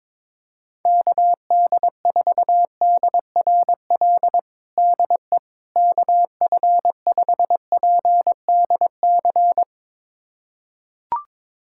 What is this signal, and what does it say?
Morse code, heard as a keyed tone, about 22 wpm, 700 Hz. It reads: KD4DRL DE KF5PDC
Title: USA Callsign DE Callsign - 6 Characters - 22wpm